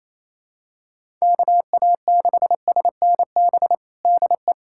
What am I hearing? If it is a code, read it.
Morse code, heard as a keyed tone, about 27 wpm, 700 Hz. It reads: KA6SNB DE
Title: USA Callsign DE Callsign - 6 Characters - 28wpm